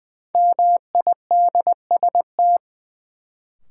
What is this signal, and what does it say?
Morse code, heard as a keyed tone, about 20 wpm, 700 Hz. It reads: MIDST
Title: Sets of 4 Words - New Word Drill - 20wpm